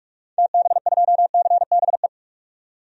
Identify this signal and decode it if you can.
Morse code, heard as a keyed tone, about 41 wpm, 700 Hz. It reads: TB2CBE
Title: DXCC-100 Call Signs - 6 Characters - 45wpm